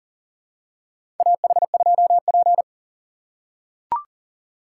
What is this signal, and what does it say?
Morse code, heard as a keyed tone, about 40 wpm, 700 Hz. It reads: AH2P
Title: USA Call Signs - 4 Characters - 6x Character Spacing - 40wpm